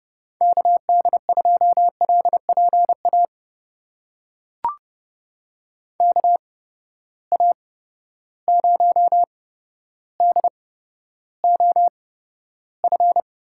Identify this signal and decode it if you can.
Morse code, heard as a keyed tone, about 30 wpm, spaced as for 13 wpm, 700 Hz. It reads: KD2LPA KA0DOF
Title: USA Call Signs - 6 Characters - 8x Character Spacing - 30wpm